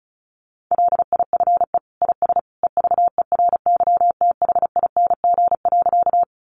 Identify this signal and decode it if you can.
Morse code, heard as a keyed tone, about 34 wpm, 700 Hz. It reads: LIFE IS EVERYTHING.
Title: Sentences from Top 700 Words 35wpm